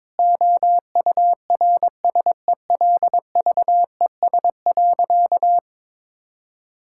Morse code, heard as a keyed tone, about 22 wpm, 700 Hz. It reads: OURSELVES.
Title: Sentences from Top 1600 Words 22wpm